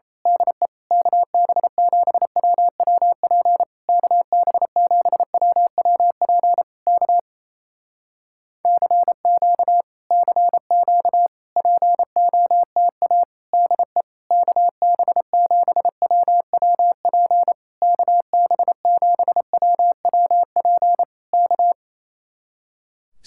Morse code, heard as a keyed tone, about 28 wpm, 700 Hz. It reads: DE KB7WWP KB7WWP K CQ CQ POTA DE KB7WWP KB7WWP K
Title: Speed-Racing Common QSO Phrases Encore - 28wpm